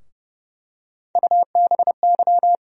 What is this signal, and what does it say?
Morse code, heard as a keyed tone, about 31 wpm, 700 Hz. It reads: UBY
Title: USA Call Sign Postfixes - 3 Letters 30wpm